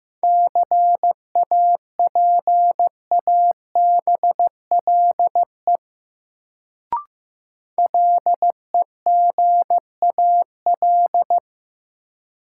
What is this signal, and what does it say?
Morse code, heard as a keyed tone, about 16 wpm, 700 Hz. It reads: CAPABLE LEGAL
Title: Sentences from Top 1200 Words - New Word Drill Alt - 15wpm